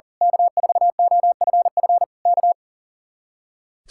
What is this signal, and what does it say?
Morse code, heard as a keyed tone, about 38 wpm, 700 Hz. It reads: K4OFF K
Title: Speed-Racing Common QSO Phrases Encore - 40wpm